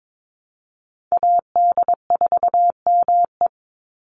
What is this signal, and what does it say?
Morse code, heard as a keyed tone, about 22 wpm, 700 Hz. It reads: AD4ME
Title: USA Call Signs 22wpm